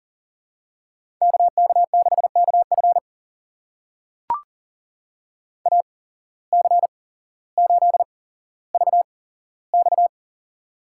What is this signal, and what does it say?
Morse code, heard as a keyed tone, about 42 wpm, spaced as for 17 wpm, 700 Hz. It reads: KK6KF AC8VX
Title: USA Call Signs - 5 Characters - 8x Character Spacing - 40wpm